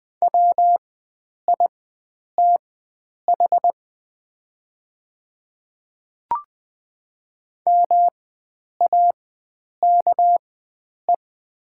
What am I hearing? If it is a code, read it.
Morse code, heard as a keyed tone, about 20 wpm, spaced as for 9 wpm, 700 Hz. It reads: WITH MAKE
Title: 4x Character Spacing - Top 100 Words - 20wpm